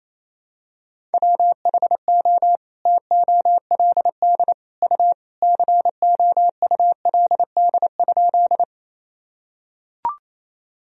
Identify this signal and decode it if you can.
Morse code, heard as a keyed tone, about 27 wpm, 700 Hz. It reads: WHO TOLD U COULD?
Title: Sentences from Top 400 Words and Abbreviations 28wpm